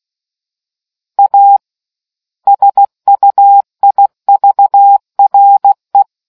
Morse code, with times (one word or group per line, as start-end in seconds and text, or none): 1.2-1.6 A
2.5-6.0 SUIVRE